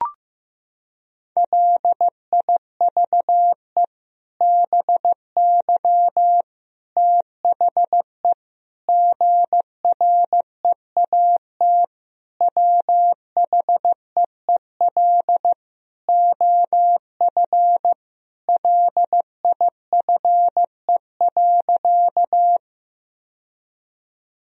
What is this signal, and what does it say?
Morse code, heard as a keyed tone, about 15 wpm, 700 Hz. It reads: LIVE BY THE GREAT WHEEL OF LIFE.
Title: Sentences from Top 1000 Words - Review Part 3 - 15wpm